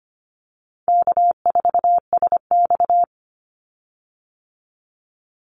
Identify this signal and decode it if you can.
Morse code, heard as a keyed tone, about 25 wpm, 700 Hz. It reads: K4SX
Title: Easy Call Signs Repeated 3x - 25wpm